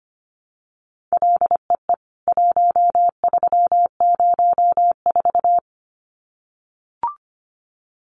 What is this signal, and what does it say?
Morse code, heard as a keyed tone, about 25 wpm, 700 Hz. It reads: LEE 1304
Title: CWT Exchange 25wpm